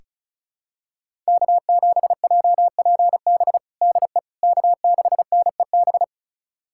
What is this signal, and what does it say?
Morse code, heard as a keyed tone, about 36 wpm, 700 Hz. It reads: K7JPB DE K6NEB
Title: USA Callsign DE Callsign 35wpm